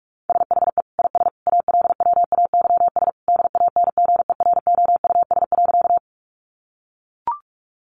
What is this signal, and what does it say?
Morse code, heard as a keyed tone, about 44 wpm, 700 Hz. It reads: SHE IS ALWAYS DANGEROUS.
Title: Sentences from Top 1200 Words 45wpm